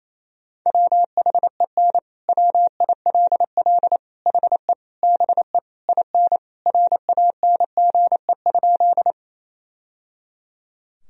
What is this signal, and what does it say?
Morse code, heard as a keyed tone, about 28 wpm, 700 Hz. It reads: WHEN WILL HE BE IN RANGE?
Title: Sentences from Top 1000 Words - Review Part 4 - 28wpm